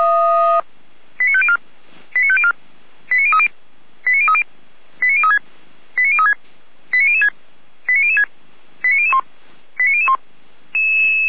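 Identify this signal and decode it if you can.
Five-tone selective call (ZVEI), heard as quick runs of five tones, tones of about 70 ms. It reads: 89504 89504 89030 89030 89036 89036 89006 89006 89001 89001
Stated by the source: ZVEI five-tone call 89099 received by SDR, fire brigade Feuerwehr Orlamünde.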